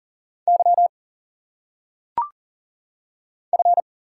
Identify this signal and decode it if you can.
Morse code, heard as a keyed tone, about 40 wpm, 700 Hz. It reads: Y F
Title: Single Letters - Phonetic Pronunciation 40wpm